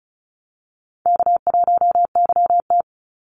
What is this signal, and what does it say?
Morse code, heard as a keyed tone, about 34 wpm, 700 Hz. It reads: K1YT
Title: Easy Call Signs Repeated 3x - 35wpm